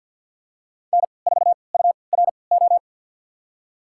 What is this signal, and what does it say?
Morse code, heard as a keyed tone, about 46 wpm, spaced as for 27 wpm, 700 Hz. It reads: N4URO
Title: USA Call Signs - 5 Characters - 3x Character Spacing - 50wpm